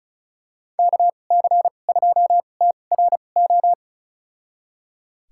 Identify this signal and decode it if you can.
Morse code, heard as a keyed tone, about 34 wpm, spaced as for 25 wpm, 700 Hz. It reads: KC2TRO
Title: USA Call Signs - 6 Characters - 2x Character Spacing - 35wpm